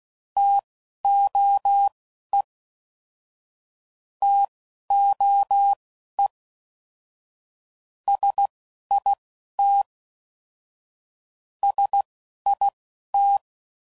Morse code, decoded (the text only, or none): TOE TOE SIT SIT